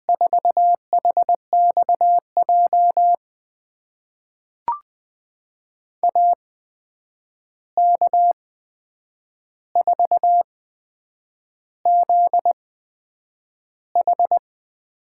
Morse code, heard as a keyed tone, about 20 wpm, spaced as for 8 wpm, 700 Hz. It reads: 4HXJ AK4ZH
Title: USA Call Signs - 5 Characters - 8x Character Spacing - 20wpm